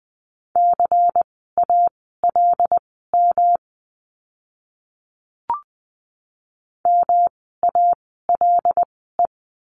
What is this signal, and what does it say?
Morse code, heard as a keyed tone, about 20 wpm, spaced as for 14 wpm, 700 Hz. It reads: CALM MALE
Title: 108 Lesson 18f - Letter m - Words 20wpm